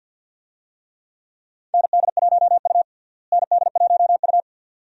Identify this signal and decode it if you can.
Morse code, heard as a keyed tone, about 52 wpm, 700 Hz. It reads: ND1U ND1U
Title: USA Call Signs - 4 Characters - Repeated 2x - 50wpm